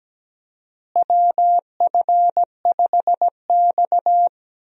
Morse code, heard as a keyed tone, about 17 wpm, 700 Hz. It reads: WF5X
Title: USA Call Signs - 4 Characters - Repeated 3x - 17wpm